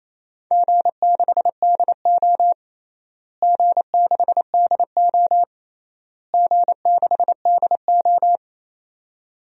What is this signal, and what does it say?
Morse code, heard as a keyed tone, about 28 wpm, 700 Hz. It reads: G6DO G6DO G6DO
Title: DXCC-100 Call Signs - 4 Characters - Repeated 3x - 28wpm